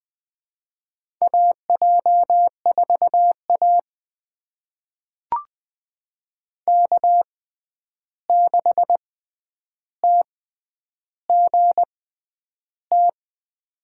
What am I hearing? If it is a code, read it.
Morse code, heard as a keyed tone, about 20 wpm, spaced as for 10 wpm, 700 Hz. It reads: AJ4A K6TGT
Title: USA Call Signs - 6x Character Spacing - 20wpm